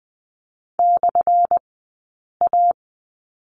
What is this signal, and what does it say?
Morse code, heard as a keyed tone, about 20 wpm, spaced as for 8 wpm, 700 Hz. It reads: /A
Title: 177 Lesson 29d - Slash - Two Characters 20wpm